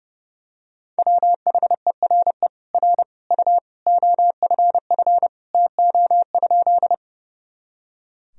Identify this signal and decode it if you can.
Morse code, heard as a keyed tone, about 30 wpm, 700 Hz. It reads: WHERE R U OFF TO?
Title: Sentences from Top 200 Words and Abbreviations 30wpm